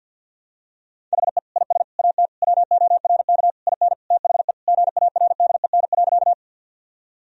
Morse code, heard as a keyed tone, about 52 wpm, 700 Hz. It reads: HE IS AT WORK IN THE GARDEN.
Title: Sentences from Top 1000 Words - Review Part 4 - 50wpm